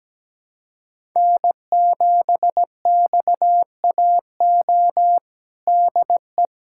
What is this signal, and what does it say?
Morse code, heard as a keyed tone, about 17 wpm, 700 Hz. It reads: N7XAO DE
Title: USA Callsign DE Callsign - 5 Characters - 17wpm